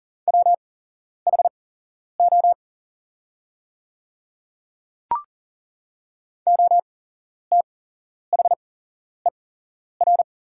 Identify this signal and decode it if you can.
Morse code, heard as a keyed tone, about 40 wpm, spaced as for 11 wpm, 700 Hz. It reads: WHO OTHER
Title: 8x Character Spacing - Top 200 Words - 40wpm